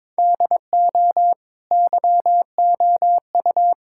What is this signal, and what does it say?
Morse code, heard as a keyed tone, about 22 wpm, 700 Hz. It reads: DO YOU
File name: practice / 22wpm / Sentences with 3-Word Phrases - 22wpm